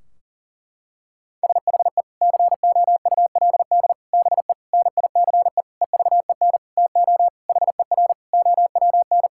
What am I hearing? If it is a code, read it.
Morse code, heard as a keyed tone, about 37 wpm, 700 Hz. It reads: SHE COULD BE NICE EVEN TO HER OWN
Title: Sentences from Top 1600 Words 40wpm